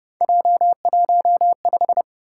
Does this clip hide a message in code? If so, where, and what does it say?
Yes: Morse code from 0.2 to 2.0 s, heard as a keyed tone, about 29 wpm, 700 Hz. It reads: J15